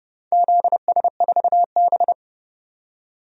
Morse code, heard as a keyed tone, about 30 wpm, 700 Hz. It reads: ZS4B
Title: DXCC-100 Call Signs - 4 Characters - 30wpm